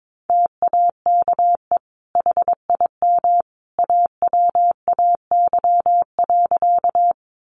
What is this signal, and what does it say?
Morse code, heard as a keyed tone, about 22 wpm, 700 Hz. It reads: TAKE HIM AWAY.
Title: Sentences from Top 600 Words 22wpm